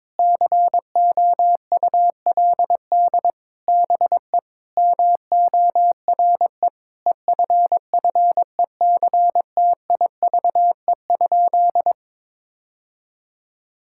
Morse code, heard as a keyed tone, about 22 wpm, 700 Hz. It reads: COULD BE MORE EFFECTIVE?
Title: Sentences from Top 1000 Words - Review Part 6 - 22wpm